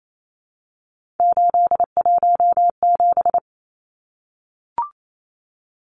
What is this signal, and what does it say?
Morse code, heard as a keyed tone, about 28 wpm, 700 Hz. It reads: 817
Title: Numbers - 3-Digit 28wpm